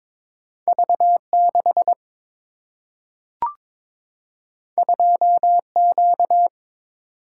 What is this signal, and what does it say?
Morse code, heard as a keyed tone, about 22 wpm, 700 Hz. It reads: V 6 2 Q
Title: Single-Letter Single-Number 22wpm